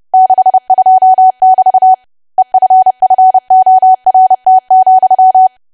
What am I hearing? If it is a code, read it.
Morse code, heard as a keyed tone, about 30 wpm, 750 Hz. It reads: 62<BT> EFFORT,
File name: cwa / W1AW-Level3 / 30WPM